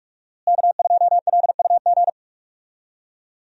Morse code, heard as a keyed tone, about 41 wpm, 700 Hz. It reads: K2LUG
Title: USA Call Signs - 5 Characters - Repeated 3x - 45wpm